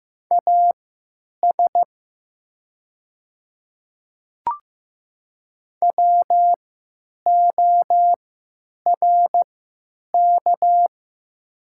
Morse code, heard as a keyed tone, about 15 wpm, spaced as for 9 wpm, 700 Hz. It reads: AS WORK
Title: 3x Character Spacing - Top 200 Words - 15wpm